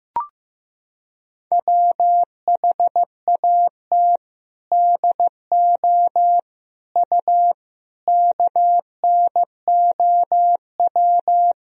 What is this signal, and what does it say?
Morse code, heard as a keyed tone, about 16 wpm, 700 Hz. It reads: WHAT DO U KNOW
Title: Sentences from Top 100 Words and Abbreviations 15wpm